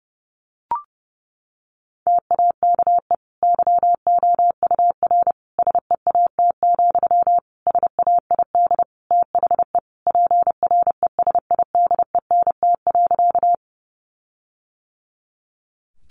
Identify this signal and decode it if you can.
Morse code, heard as a keyed tone, about 30 wpm, 700 Hz. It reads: TAKE YOUR SEAT, SAID THE PRESIDENT.
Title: Sentences from Top 1500 Words 30wpm